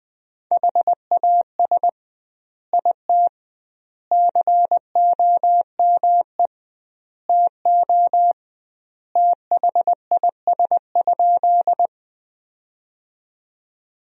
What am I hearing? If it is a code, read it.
Morse code, heard as a keyed tone, about 20 wpm, 700 Hz. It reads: HAS IT COME TO THIS?
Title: 2x Word-Spacing - Sentences from Top 100 Words - 20wpm